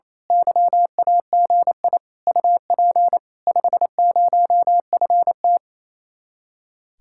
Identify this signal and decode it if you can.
Morse code, heard as a keyed tone, about 26 wpm, 700 Hz. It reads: YAGI UP 50FT
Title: Speed-Racing Common QSO Phrases Encore - 28wpm